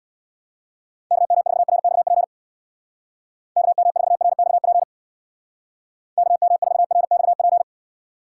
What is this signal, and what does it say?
Morse code, heard as a keyed tone, about 65 wpm, 700 Hz. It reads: KM4RXP KM4RXP KM4RXP